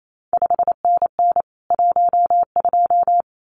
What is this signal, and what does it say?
Morse code, heard as a keyed tone, about 28 wpm, 700 Hz. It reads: 5NN 12
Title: CQ WW DX Contest Exchange Encore - 28wpm